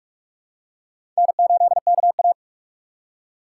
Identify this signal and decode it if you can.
Morse code, heard as a keyed tone, about 41 wpm, 700 Hz. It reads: N8KA
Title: USA Call Signs - 4 Characters - 2x Character Spacing - 45wpm